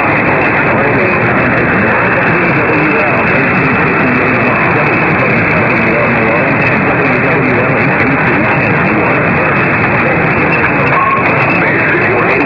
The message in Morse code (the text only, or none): none